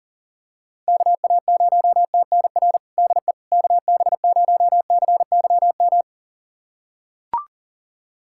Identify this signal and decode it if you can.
Morse code, heard as a keyed tone, about 37 wpm, 700 Hz. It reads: KA0TNR DE KB0CYM
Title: USA Callsign DE Callsign 40wpm